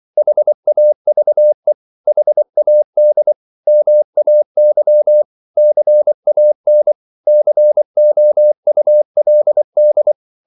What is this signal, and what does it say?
Morse code, heard as a keyed tone, about 24 wpm, 600 Hz. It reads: HAVE HAD MAY CAN COULD
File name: CW / Practice / 100MostCommonEnglishWords24